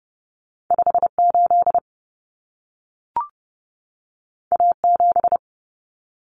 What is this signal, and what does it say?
Morse code, heard as a keyed tone, about 30 wpm, 700 Hz. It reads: 58 A7
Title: Common Cut Numbers - 2-Digit 30wpm